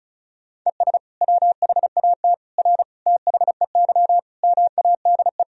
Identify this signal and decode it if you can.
Morse code, heard as a keyed tone, about 36 wpm, 700 Hz. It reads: ES WHAT R THEY MADE
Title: Sentences from Top 200 Words and Abbreviations 35wpm